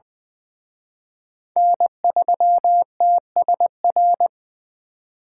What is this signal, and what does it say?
Morse code, heard as a keyed tone, about 24 wpm, 700 Hz. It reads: N3TSR
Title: Speed Racing - USA Call Signs 20wpm